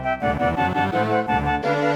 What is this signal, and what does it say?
There is no code.